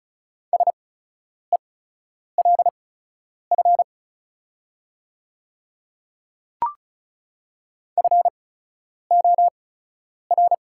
Morse code, heard as a keyed tone, about 34 wpm, spaced as for 10 wpm, 690 Hz. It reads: SELF FOR
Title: 8x Character Spacing - Top 200 Words - 35wpm